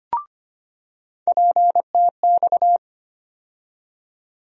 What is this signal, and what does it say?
Morse code, heard as a keyed tone, about 25 wpm, 700 Hz. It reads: PTX
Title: USA Call Sign Postfixes - 3 Letters 25wpm